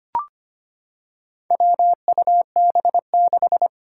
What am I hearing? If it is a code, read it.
Morse code, heard as a keyed tone, about 25 wpm, 700 Hz. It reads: WUB6
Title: Four Challenging Letter-Number Combinations 25wpm